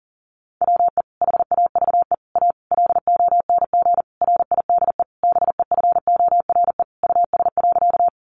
Morse code, heard as a keyed tone, about 40 wpm, 700 Hz. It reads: WE HAVE A LONG RIDE BEFORE US.
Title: Sentences from Top 1200 Words 40wpm